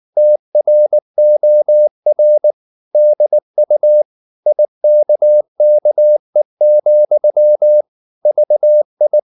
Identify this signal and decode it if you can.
Morse code, heard as a keyed tone, about 19 wpm, 600 Hz. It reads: TROR DU IKKE, VI
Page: Ferie 19 wpm